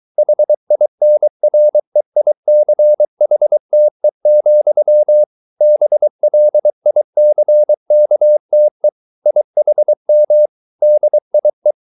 Morse code, heard as a keyed tone, about 23 wpm, 600 Hz. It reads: HINREICHTE, BLICKTE IHM DIE